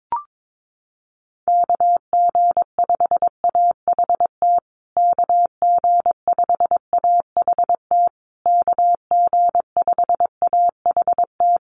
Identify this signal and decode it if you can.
Morse code, heard as a keyed tone, about 22 wpm, 700 Hz. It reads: KG5AHT KG5AHT KG5AHT
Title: USA Call Signs Repeated 3x 22wpm